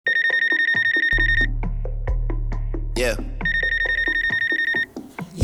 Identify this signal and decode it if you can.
Speech. There is no code.